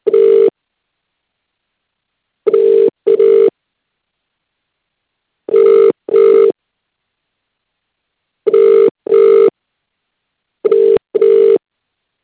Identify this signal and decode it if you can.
Ringback tone (UK), 400 + 450 Hz.